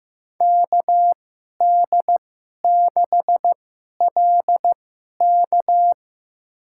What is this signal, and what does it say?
Morse code, heard as a keyed tone, about 15 wpm, spaced as for 11 wpm, 700 Hz. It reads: KD6LK
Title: USA Call Signs - 5 Characters - 2x Character Spacing - 15wpm